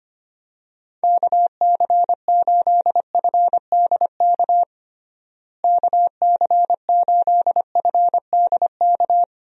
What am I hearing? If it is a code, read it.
Morse code, heard as a keyed tone, about 25 wpm, 700 Hz. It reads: KC8FDK KC8FDK
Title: USA Call Signs - 6 Characters - Repeated 3x - 25wpm